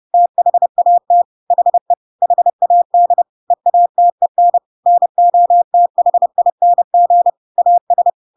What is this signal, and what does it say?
Morse code, heard as a keyed tone, about 30 wpm, 700 Hz. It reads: THAT HE HAD EATEN NOTHING AS